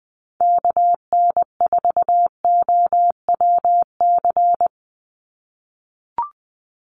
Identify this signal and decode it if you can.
Morse code, heard as a keyed tone, about 20 wpm, 700 Hz. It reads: KN4OWC
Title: USA Call Signs 20wpm